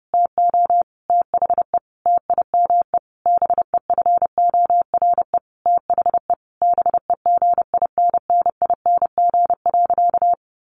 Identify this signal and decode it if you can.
Morse code, heard as a keyed tone, about 30 wpm, 700 Hz. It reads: TO THE TIME BEFORE THE BEGINNING.
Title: A Sound Of Thunder 30wpm